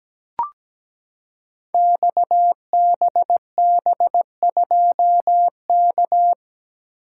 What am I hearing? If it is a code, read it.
Morse code, heard as a keyed tone, about 17 wpm, 700 Hz. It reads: XBB2K
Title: Code Groups - 5 Letter:Numbers - 17wpm